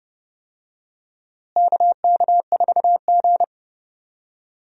Morse code, heard as a keyed tone, about 30 wpm, 700 Hz. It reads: KK4G
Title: USA Call Signs - 4 Characters - Repeated 3x - 30wpm